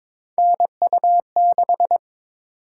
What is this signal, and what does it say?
Morse code, heard as a keyed tone, about 22 wpm, 700 Hz. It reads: NU6
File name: practice / 22wpm / Three Letter-Number Combinations - 22wpm